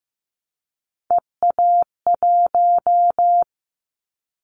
Morse code, heard as a keyed tone, about 15 wpm, 700 Hz. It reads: EA1
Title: Callsign Prefixes for Top 25 DXCC - 15wpm